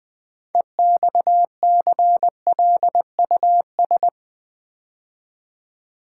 Morse code, heard as a keyed tone, about 20 wpm, 700 Hz. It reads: EXCLUS
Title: Build A Word - 3 Letters At A Time - 9-Letter Words - 20wpm